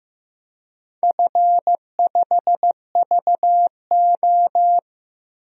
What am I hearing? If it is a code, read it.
Morse code, heard as a keyed tone, about 15 wpm, 700 Hz. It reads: F5VO
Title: DXCC-100 Call Signs - 4 Characters - 15wpm